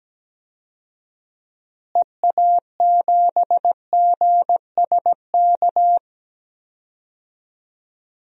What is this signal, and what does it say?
Morse code, heard as a keyed tone, about 17 wpm, 700 Hz. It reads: EA7GSK
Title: DXCC-100 Call Signs - 6 Characters - Repeated 3x - 17wpm